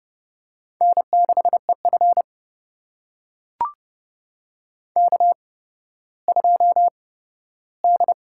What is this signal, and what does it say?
Morse code, heard as a keyed tone, about 30 wpm, spaced as for 15 wpm, 700 Hz. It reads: N6EF K2D
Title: USA Call Signs - 8x Character Spacing - 30wpm